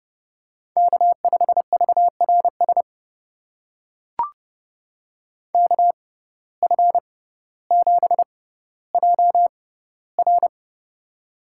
Morse code, heard as a keyed tone, about 30 wpm, spaced as for 15 wpm, 700 Hz. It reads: K5VRS KF7JR
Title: USA Call Signs - 5 Characters - 6x Character Spacing - 30wpm